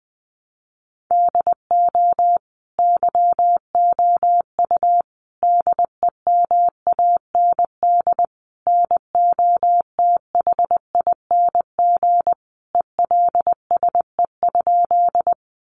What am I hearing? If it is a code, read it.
Morse code, heard as a keyed tone, about 20 wpm, 700 Hz. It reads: DO YOU DEMAND NOTHING ELSE?
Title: Sentences from Top 800 Words 20wpm